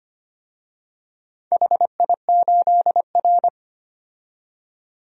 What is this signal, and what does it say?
Morse code, heard as a keyed tone, about 25 wpm, 700 Hz. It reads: HI8R
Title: DXCC-100 Call Signs - 4 Characters - 25wpm